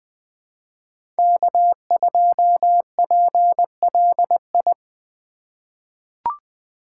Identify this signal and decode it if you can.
Morse code, heard as a keyed tone, about 20 wpm, 700 Hz. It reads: K2PLI
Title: USA Call Signs - 5 Characters - 2x Character Spacing - 20wpm